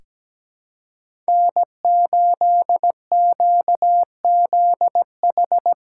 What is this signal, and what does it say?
Morse code, heard as a keyed tone, about 17 wpm, 700 Hz. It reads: N8QZH
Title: USA Callsign DE Callsign - 5 Characters - 17wpm